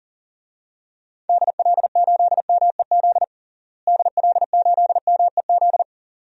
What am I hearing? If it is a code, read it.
Morse code, heard as a keyed tone, about 37 wpm, 700 Hz. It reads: DL8MEZ DL8MEZ
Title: DXCC-100 Call Signs - 6 Characters - Repeated 2x - 40wpm